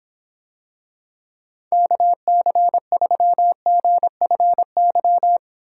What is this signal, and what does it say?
Morse code, heard as a keyed tone, about 26 wpm, 700 Hz. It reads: KC3GFY